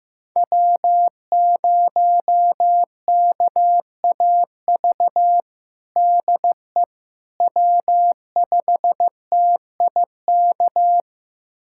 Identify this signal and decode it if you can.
Morse code, heard as a keyed tone, about 15 wpm, 700 Hz. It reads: W0KAV DE W5TIK
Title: USA Callsign DE Callsign - 5 Characters - 15wpm